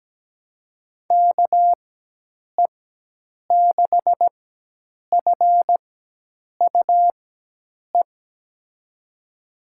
Morse code, heard as a keyed tone, about 17 wpm, spaced as for 8 wpm, 700 Hz. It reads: KE6FUE
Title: USA Call Signs - 6 Characters - 4x Character Spacing - 17wpm